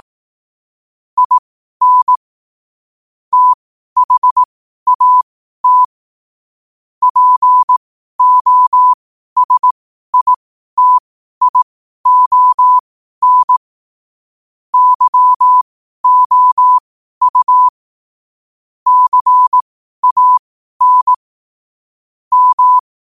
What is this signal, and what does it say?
Morse code, heard as a keyed tone, about 18 wpm, spaced as for 12 wpm, 1000 Hz. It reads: IN THAT POSITION YOU CAN M
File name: qotdfiles / 12wpm